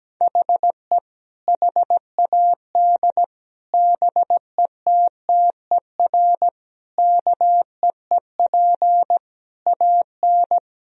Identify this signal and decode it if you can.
Morse code, heard as a keyed tone, about 17 wpm, 700 Hz. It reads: HE HAD BETTER KEEP AN
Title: Sentences from Top 1000 Words - Review Part 1 - 17wpm